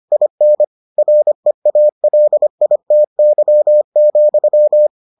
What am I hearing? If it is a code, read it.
Morse code, heard as a keyed tone, about 25 wpm, 600 Hz. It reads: IN REALITY,